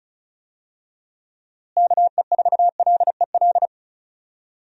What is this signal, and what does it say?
Morse code, heard as a keyed tone, about 34 wpm, 700 Hz. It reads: KE4LEL
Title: USA Call Signs - 6 Characters - Repeated 3x - 35wpm